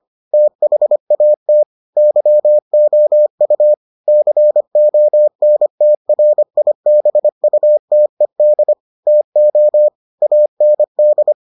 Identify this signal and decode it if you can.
Morse code, heard as a keyed tone, about 25 wpm, 600 Hz. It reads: THAT YOU CONTRIBUTED TO AND